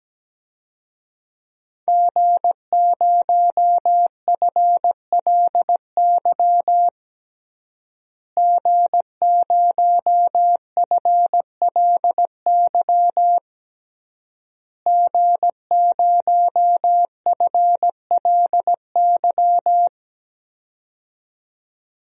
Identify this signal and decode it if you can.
Morse code, heard as a keyed tone, about 17 wpm, 700 Hz. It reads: G0FLY G0FLY G0FLY
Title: DXCC-100 Call Signs - 5 Characters - Repeated 3x - 17wpm